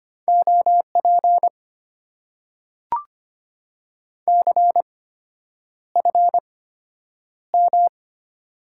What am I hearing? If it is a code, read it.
Morse code, heard as a keyed tone, about 25 wpm, spaced as for 9 wpm, 700 Hz. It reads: OP CFM
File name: Top 90 QSO Elements - 8x Character Spacing - 25wpm